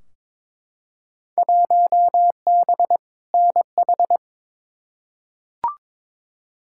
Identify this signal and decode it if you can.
Morse code, heard as a keyed tone, about 22 wpm, 700 Hz. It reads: 1B NH
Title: Field Day Exchanges Encore - 22wpm